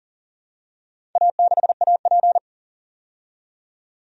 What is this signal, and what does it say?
Morse code, heard as a keyed tone, about 40 wpm, 700 Hz. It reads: A6AP
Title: DXCC-100 Call Signs - 4 Characters - 40wpm